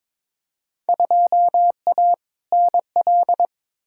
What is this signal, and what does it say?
Morse code, heard as a keyed tone, about 22 wpm, 700 Hz. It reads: 2A NL
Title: Field Day Exchanges Encore - 22wpm